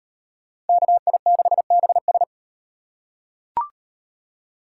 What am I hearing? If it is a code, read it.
Morse code, heard as a keyed tone, about 36 wpm, 700 Hz. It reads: KI6BS